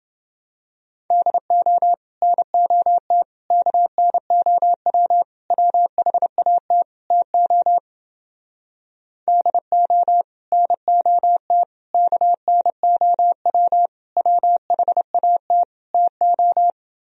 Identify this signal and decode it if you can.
Morse code, heard as a keyed tone, about 30 wpm, 700 Hz. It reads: DO NOT KNOW WHAT TO DO NOT KNOW WHAT TO